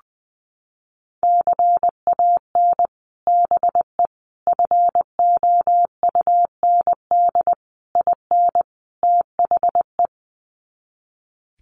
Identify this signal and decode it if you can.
Morse code, heard as a keyed tone, about 20 wpm, 700 Hz. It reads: CAN BE FOUND IN THE
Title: Sets of 5 Words Encore 20wpm